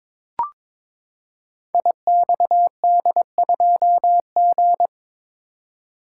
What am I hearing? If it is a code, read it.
Morse code, heard as a keyed tone, about 22 wpm, 700 Hz. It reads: IXD2G
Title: Code Groups - 5 Letter:Numbers - 22wpm